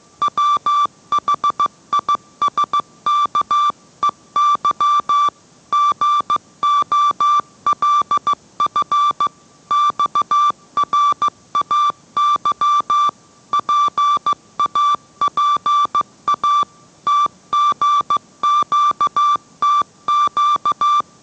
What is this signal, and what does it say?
Morse code, heard as a keyed tone, about 15 wpm, 1170 Hz. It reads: WHISKEY GOLF XRAY PAPA TGQTQ